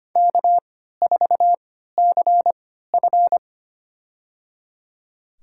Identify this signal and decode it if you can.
Morse code, heard as a keyed tone, about 25 wpm, spaced as for 14 wpm, 700 Hz. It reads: K4CF
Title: USA Call Signs - 3x Character Spacing - 25wpm